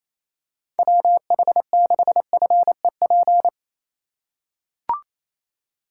Morse code, heard as a keyed tone, about 28 wpm, 700 Hz. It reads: WH6FEP
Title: USA Call Signs - 6 Characters - 3x Character Spacing - 28wpm